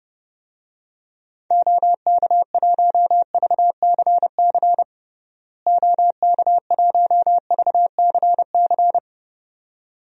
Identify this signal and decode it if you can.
Morse code, heard as a keyed tone, about 30 wpm, 700 Hz. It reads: OK1VCC OK1VCC
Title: DXCC-100 Call Signs - 6 Characters - Repeated 2x - 30wpm